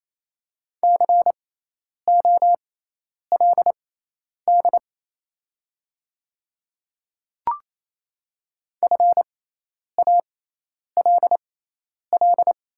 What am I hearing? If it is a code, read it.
Morse code, heard as a keyed tone, about 28 wpm, spaced as for 10 wpm, 690 Hz. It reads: COLD FALL
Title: 6x Character Spacing - Top 500 Words - 28wpm